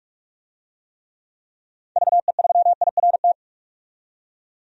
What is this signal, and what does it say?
Morse code, heard as a keyed tone, about 46 wpm, 700 Hz. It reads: VE3IRT
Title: DXCC-100 Call Signs - 6 Characters - 45wpm